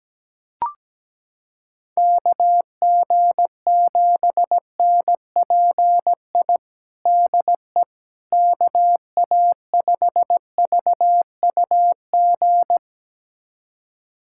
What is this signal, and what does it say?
Morse code, heard as a keyed tone, about 17 wpm, 700 Hz. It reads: KG7NPI DE KA5VUG
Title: USA Callsign DE Callsign - 6 Characters - 17wpm